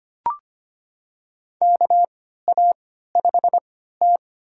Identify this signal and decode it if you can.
Morse code, heard as a keyed tone, about 25 wpm, spaced as for 14 wpm, 700 Hz. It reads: KA5T
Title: USA Call Signs - 4 Characters - 3x Character Spacing - 25wpm